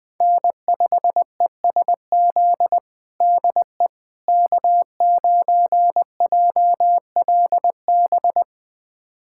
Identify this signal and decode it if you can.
Morse code, heard as a keyed tone, about 20 wpm, 700 Hz. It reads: N5ESZ DE K9JLB
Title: USA Callsign DE Callsign - 5 Characters - 20wpm